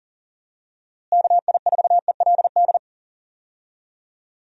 Morse code, heard as a keyed tone, about 40 wpm, 700 Hz. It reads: KI4ELD
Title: USA Call Signs - 6 Characters - Repeated 3x - 40wpm